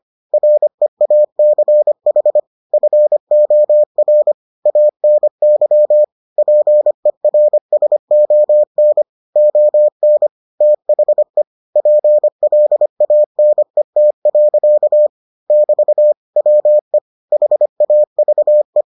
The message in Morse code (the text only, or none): REACH FOR ANY PERSON ON THE PLANET. <BT> WE HAVE